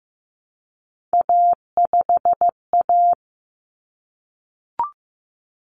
Morse code, heard as a keyed tone, about 15 wpm, 700 Hz. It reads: A5A
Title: Common Cut Numbers - 3-Digit 15wpm